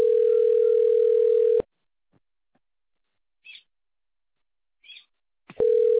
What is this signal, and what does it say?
Ringback tone (North America), 440 + 480 Hz.